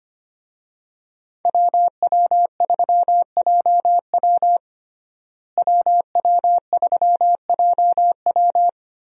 Morse code, heard as a keyed tone, about 25 wpm, 700 Hz. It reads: WW3JW WW3JW
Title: USA Call Signs - 5 Characters - Repeated 3x - 25wpm